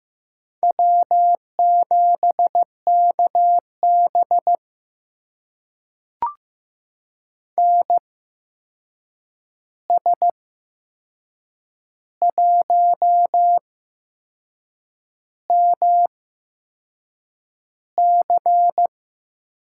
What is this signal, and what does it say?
Morse code, heard as a keyed tone, about 15 wpm, spaced as for 6 wpm, 700 Hz. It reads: W7KB NS1MC
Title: USA Call Signs - 8x Character Spacing - 15wpm